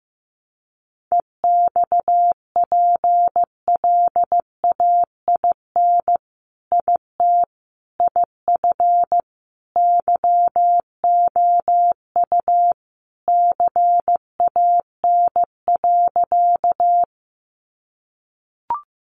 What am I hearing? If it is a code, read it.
Morse code, heard as a keyed tone, about 15 wpm, 700 Hz. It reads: EXPLAIN IT IF YOU CAN.
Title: Sentences from Top 1100 Words 15wpm